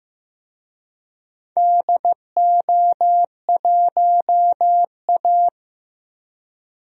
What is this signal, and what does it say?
Morse code, heard as a keyed tone, about 16 wpm, 700 Hz. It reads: DO1A
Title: DXCC-100 Call Signs - 4 Characters - Repeated 3x - 15wpm